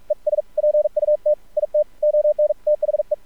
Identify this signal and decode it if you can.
Morse code, heard as a keyed tone, about 46 wpm, 600 Hz. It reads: ES PUT IT ON THE
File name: archived / CRSnet